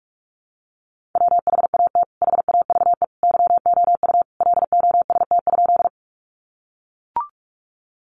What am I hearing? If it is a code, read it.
Morse code, heard as a keyed tone, about 44 wpm, 700 Hz. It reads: WHAT HAVE YOU LOST?
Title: Sentences from Top 1400 Words 45wpm